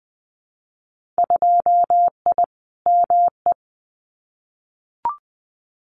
Morse code, heard as a keyed tone, about 20 wpm, 700 Hz. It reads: 2I ME
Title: Winter Field Day Exchanges Encore 20wpm